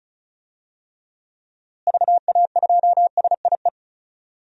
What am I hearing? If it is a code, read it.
Morse code, heard as a keyed tone, about 34 wpm, 700 Hz. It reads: VA2SIE
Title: DXCC-100 Call Signs - 6 Characters - Repeated 3x - 35wpm